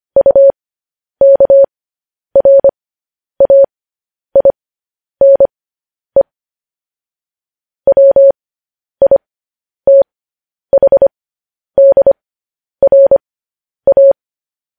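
Morse code, heard as a keyed tone, about 25 wpm, spaced as for 10 wpm, 550 Hz. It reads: UKRAINE WITHDRA
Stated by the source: Morse code news headlines sent at 10 wpm.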